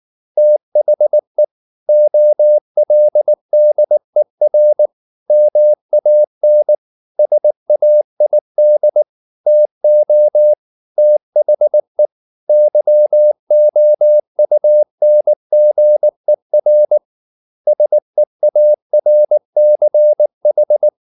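Morse code, heard as a keyed tone, about 19 wpm, 600 Hz. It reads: THE OLDER MAN SAID TO THE YOUNGER SEARCH
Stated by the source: Morse code at 19 wpm from CW med Gnister, reading Ken Follett's novel Never in English.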